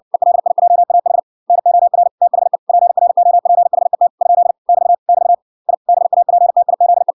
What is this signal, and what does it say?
Morse code, heard as a keyed tone, about 56 wpm, 700 Hz. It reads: ELIJAH NOR THE PROPHET ? <BT> <BT> I BAPTIZE